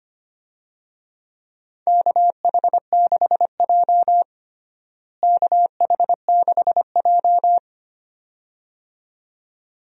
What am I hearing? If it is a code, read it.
Morse code, heard as a keyed tone, about 25 wpm, 700 Hz. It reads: KH6J KH6J
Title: DXCC-100 Call Signs - 4 Characters - Repeated 2x - 25wpm